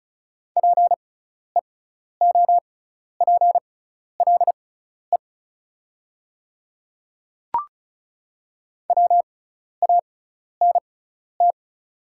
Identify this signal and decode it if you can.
Morse code, heard as a keyed tone, about 36 wpm, spaced as for 12 wpm, 690 Hz. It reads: PEOPLE WANT
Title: 6x Character Spacing - Top 200 Words - 35wpm